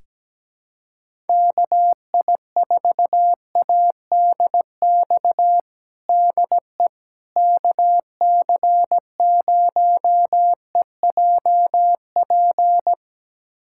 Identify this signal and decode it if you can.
Morse code, heard as a keyed tone, about 17 wpm, 700 Hz. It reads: KI4ADX DE KC0EJP
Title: USA Callsign DE Callsign - 6 Characters - 17wpm